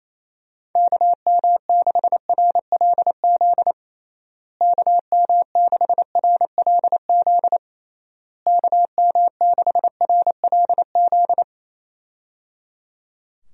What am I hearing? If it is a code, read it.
Morse code, heard as a keyed tone, about 28 wpm, 700 Hz. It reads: KM6RLZ KM6RLZ KM6RLZ
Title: USA Call Signs - 6 Characters - Repeated 3x - 28wpm